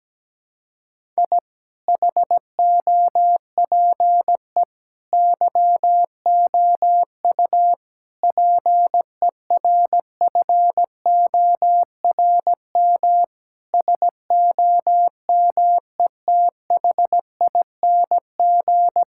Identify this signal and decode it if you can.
Morse code, heard as a keyed tone, about 17 wpm, 700 Hz. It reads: I HOPE YOU PERFORM SOMETHING
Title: Sentences from Top 1000 Words - Review Part 6 - 17wpm